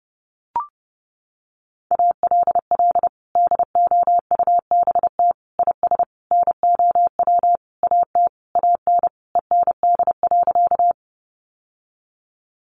Morse code, heard as a keyed tone, about 30 wpm, 700 Hz. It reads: ALL DOUBT IS NOW AT AN END.
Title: Sentences from Top 1500 Words 30wpm